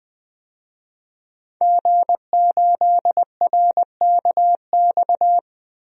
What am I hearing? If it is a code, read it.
Morse code, heard as a keyed tone, about 20 wpm, 700 Hz. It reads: G8RKX
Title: DXCC-100 Call Signs - 5 Characters - 20wpm